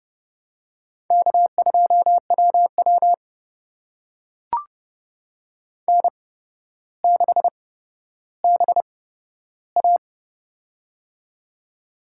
Morse code, heard as a keyed tone, about 30 wpm, spaced as for 13 wpm, 700 Hz. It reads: K2WW N6BA
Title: USA Call Signs - 4 Characters - 8x Character Spacing - 30wpm